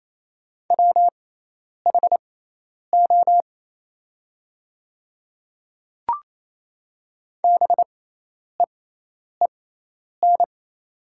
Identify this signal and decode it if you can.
Morse code, heard as a keyed tone, about 27 wpm, spaced as for 10 wpm, 690 Hz. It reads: WHO BEEN
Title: 6x Character Spacing - Top 100 Words - 28wpm